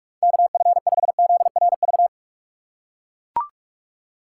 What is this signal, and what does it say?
Morse code, heard as a keyed tone, about 44 wpm, 700 Hz. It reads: KF5ZRV
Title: USA Call Signs - 6 Characters - 2x Character Spacing - 45wpm